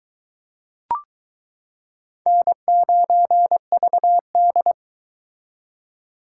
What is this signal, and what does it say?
Morse code, heard as a keyed tone, about 22 wpm, 700 Hz. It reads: N9VD